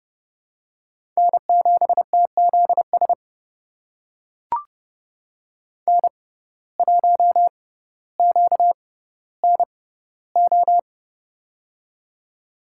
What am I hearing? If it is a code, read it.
Morse code, heard as a keyed tone, about 30 wpm, spaced as for 15 wpm, 700 Hz. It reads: N7TZS N1QNO
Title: USA Call Signs - 5 Characters - 6x Character Spacing - 30wpm